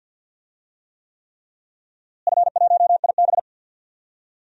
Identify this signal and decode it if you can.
Morse code, heard as a keyed tone, about 47 wpm, 700 Hz. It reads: F1IB
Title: DXCC-100 Call Signs - 4 Characters - Repeated 3x - 50wpm